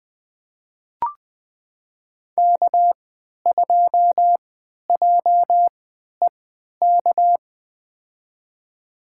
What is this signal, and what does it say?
Morse code, heard as a keyed tone, about 20 wpm, spaced as for 11 wpm, 700 Hz. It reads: K2JEK
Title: USA Call Signs - 5 Characters - 3x Character Spacing - 20wpm